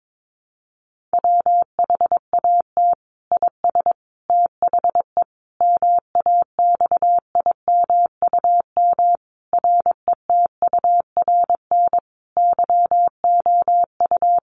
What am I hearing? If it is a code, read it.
Morse code, heard as a keyed tone, about 22 wpm, 700 Hz. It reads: WHAT IS THE MAXIMUM RETURN YOU
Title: Sentences from Top 1300 Words 22wpm